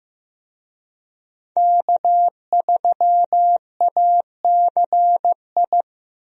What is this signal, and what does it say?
Morse code, heard as a keyed tone, about 15 wpm, 700 Hz. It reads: K3ACI
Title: USA Call Signs - 5 Characters - Repeated 2x - 15wpm